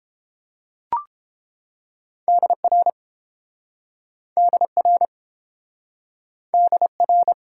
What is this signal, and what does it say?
Morse code, heard as a keyed tone, about 30 wpm, 700 Hz. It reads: DR DR DR